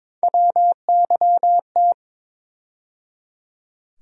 Morse code, heard as a keyed tone, about 21 wpm, 700 Hz. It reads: WYT
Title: USA Call Sign Postfixes - 3 Letters 22wpm